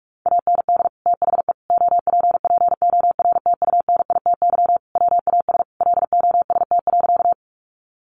Morse code, heard as a keyed tone, about 44 wpm, 700 Hz. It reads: AND THE OPPORTUNITY WAS LOST.
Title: Sentences from Top 1400 Words 45wpm